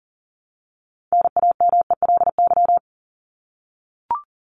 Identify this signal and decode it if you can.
Morse code, heard as a keyed tone, about 40 wpm, 700 Hz. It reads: NAMELY
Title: LY Suffix - Top 100 Words with Base Words 40wpm